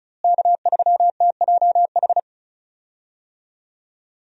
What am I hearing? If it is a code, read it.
Morse code, heard as a keyed tone, about 36 wpm, 700 Hz. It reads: K3TJH
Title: USA Call Signs - 5 Characters - Repeated 3x - 35wpm